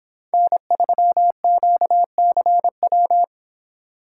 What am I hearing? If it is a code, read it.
Morse code, heard as a keyed tone, about 26 wpm, 700 Hz. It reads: N3QCW